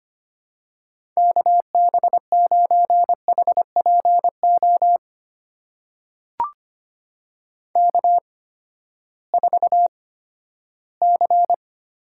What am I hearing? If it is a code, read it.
Morse code, heard as a keyed tone, about 25 wpm, spaced as for 14 wpm, 700 Hz. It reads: KB9HPO K4C
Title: USA Call Signs - 8x Character Spacing - 25wpm